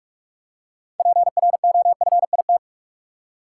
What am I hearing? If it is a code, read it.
Morse code, heard as a keyed tone, about 46 wpm, 700 Hz. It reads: PROFIT
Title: Six Letter Words - 45wpm